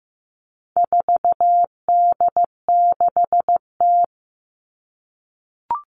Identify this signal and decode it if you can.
Morse code, heard as a keyed tone, about 15 wpm, 700 Hz. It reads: 4D6T
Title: All Cut Numbers - 4-Digit 15wpm